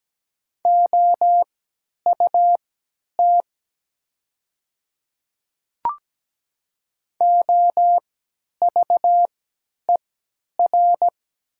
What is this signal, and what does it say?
Morse code, heard as a keyed tone, about 17 wpm, spaced as for 10 wpm, 700 Hz. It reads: OUT OVER
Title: 3x Character Spacing - Top 100 Words - 17wpm